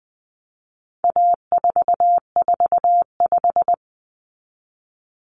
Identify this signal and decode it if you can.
Morse code, heard as a keyed tone, about 20 wpm, 700 Hz. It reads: A445
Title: Common Cut Numbers - 4-Digit 20wpm